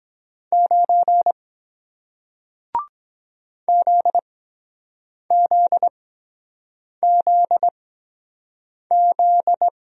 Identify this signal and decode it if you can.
Morse code, heard as a keyed tone, about 23 wpm, spaced as for 7 wpm, 700 Hz. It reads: 9 ZZZZ